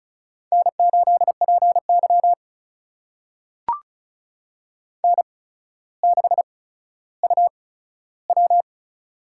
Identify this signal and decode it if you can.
Morse code, heard as a keyed tone, about 35 wpm, spaced as for 15 wpm, 700 Hz. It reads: N8PY N6UW
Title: USA Call Signs - 4 Characters - 8x Character Spacing - 35wpm